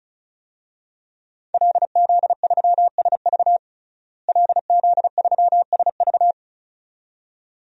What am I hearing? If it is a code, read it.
Morse code, heard as a keyed tone, about 36 wpm, 700 Hz. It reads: LZ3SV LZ3SV
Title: DXCC-100 Call Signs - 5 Characters - Repeated 2x - 35wpm